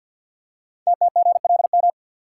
Morse code, heard as a keyed tone, about 52 wpm, 700 Hz. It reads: TTGLM
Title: Code Groups - 5 Letters - 50wpm